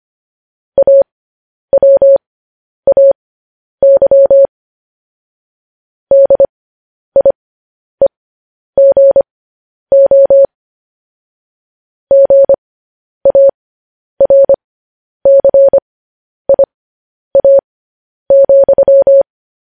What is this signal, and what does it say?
Morse code, heard as a keyed tone, about 25 wpm, spaced as for 10 wpm, 550 Hz. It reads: AWAY DIEGO GARCIA,